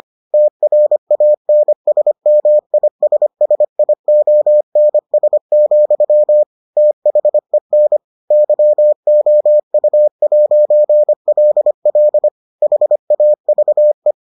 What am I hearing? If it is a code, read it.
Morse code, heard as a keyed tone, about 25 wpm, 600 Hz. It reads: TRANSMISSIONS, THEN YOU'LL HAVE